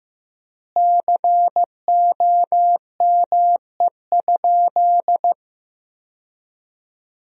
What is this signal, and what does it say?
Morse code, heard as a keyed tone, about 15 wpm, 700 Hz. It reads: COME?
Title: 3x Word-Spacing - Sentences from Top 200 Words - 15wpm